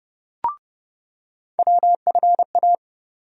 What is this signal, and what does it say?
Morse code, heard as a keyed tone, about 30 wpm, 700 Hz. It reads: WFA
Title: USA Call Sign Postfixes - 3 Letters 30wpm